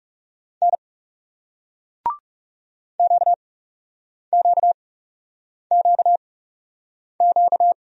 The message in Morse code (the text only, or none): N QQQQ